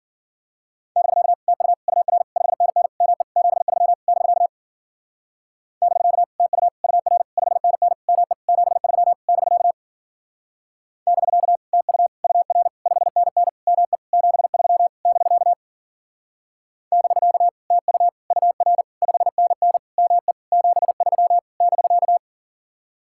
Morse code, heard as a keyed tone, about 49 wpm, 700 Hz. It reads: <BK> TU UR 5NN ME 73 <BK> <BK> TU UR 5NN ME 73 <BK> <BK> TU UR 5NN ME 73 <BK> <BK> TU UR 5NN ME 73 <BK>
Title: Speed-Racing Common QSO Phrases - 40wpm